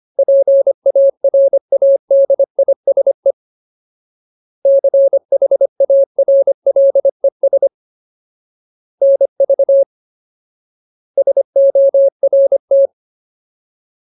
Morse code, heard as a keyed tone, about 25 wpm, 550 Hz. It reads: PARADISE CHARLES NV SORT
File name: Ultimate Word List / 25WPM Ultimate